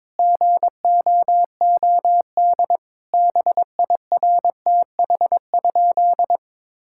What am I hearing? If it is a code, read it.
Morse code, heard as a keyed tone, about 22 wpm, 700 Hz. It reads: GOOD BIRTH?
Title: Sentences from Top 1000 Words - Review Part 5 - 22wpm